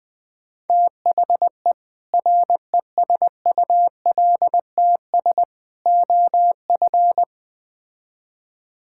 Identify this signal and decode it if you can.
Morse code, heard as a keyed tone, about 20 wpm, 700 Hz. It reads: THE RESULTS OF
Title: Build A Phrase - 5-Word Phrases - 20wpm